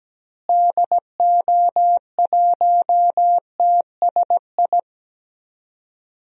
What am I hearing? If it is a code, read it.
Morse code, heard as a keyed tone, about 17 wpm, 700 Hz. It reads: DO1TSI
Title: DXCC-100 Call Signs - 6 Characters - Repeated 3x - 17wpm